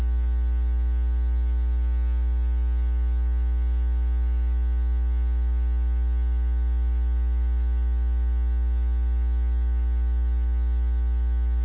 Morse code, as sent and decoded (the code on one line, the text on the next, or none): none